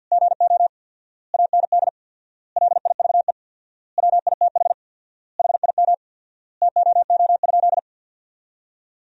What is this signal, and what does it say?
Morse code, heard as a keyed tone, about 51 wpm, 700 Hz. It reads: GO AND LIVE WITH HIM TOO?
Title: 4x Word-Spacing - Sentences from Top 500 Words - 50wpm